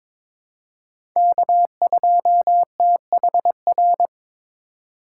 Morse code, heard as a keyed tone, about 22 wpm, 700 Hz. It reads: K2THR
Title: USA Call Signs - 5 Characters - 3x Character Spacing - 22wpm